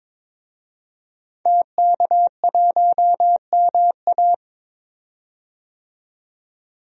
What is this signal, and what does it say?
Morse code, heard as a keyed tone, about 22 wpm, 700 Hz. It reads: TK1MA
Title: DXCC-100 Call Signs - 5 Characters - 22wpm